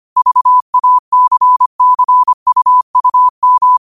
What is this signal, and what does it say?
Morse code, heard as a keyed tone, about 25 wpm, 1000 Hz. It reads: VACCUUM
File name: qotdfiles / 25wpm